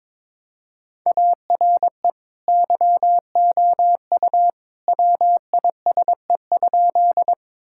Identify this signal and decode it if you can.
Morse code, heard as a keyed tone, about 21 wpm, 700 Hz. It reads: ARE YOU WISE?
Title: Sentences from Top 1600 Words 22wpm